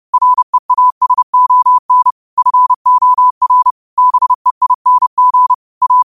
Morse code, heard as a keyed tone, about 30 wpm, 1000 Hz. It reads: REASON FOR BEING A